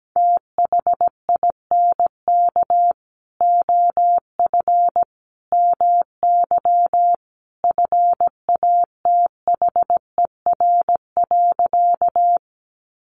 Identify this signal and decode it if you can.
Morse code, heard as a keyed tone, about 17 wpm, 700 Hz. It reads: THINK OF MY FATHER.
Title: Sentences from Top 300 Words 17wpm